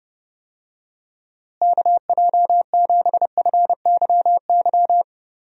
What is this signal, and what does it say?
Morse code, heard as a keyed tone, about 30 wpm, 700 Hz. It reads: KJ7FYY
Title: USA Call Signs - 6 Characters - Repeated 3x - 30wpm